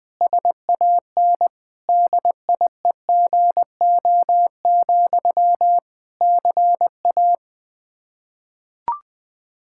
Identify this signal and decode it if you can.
Morse code, heard as a keyed tone, about 20 wpm, 700 Hz. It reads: SAN DIEGO, CA
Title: Practice QSO - Exchange 1 - Beginning - 20wpm